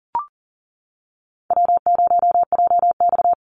Morse code, heard as a keyed tone, about 40 wpm, 700 Hz. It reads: W0JX
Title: Easy Call Signs - 40wpm